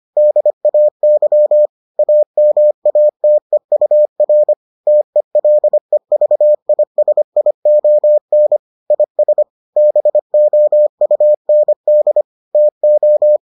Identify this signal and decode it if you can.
Morse code, heard as a keyed tone, about 25 wpm, 600 Hz. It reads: DAY AMATEUR TELEVISION IS BOUND TO